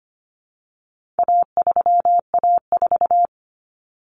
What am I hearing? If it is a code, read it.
Morse code, heard as a keyed tone, about 25 wpm, 700 Hz. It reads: A3A4
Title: Common Cut Numbers - 4-Digit 25wpm